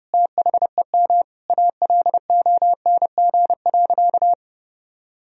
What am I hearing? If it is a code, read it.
Morse code, heard as a keyed tone, about 29 wpm, 700 Hz. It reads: THEM ALONG.
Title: Sentences from Top 1000 Words - Review Part 3 - 30wpm